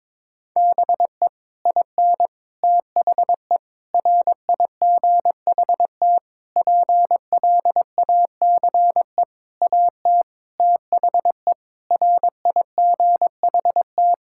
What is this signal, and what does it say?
Morse code, heard as a keyed tone, about 22 wpm, 700 Hz. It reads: BE IN THE RIGHT PLACE AT THE RIGHT
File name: Sets of 9 Words - 22wpm